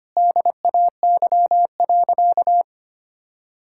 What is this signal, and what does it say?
Morse code, heard as a keyed tone, about 25 wpm, 700 Hz. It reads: DAY.
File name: practice / 25wpm / Sentences with 3-Word Phrases - 25wpm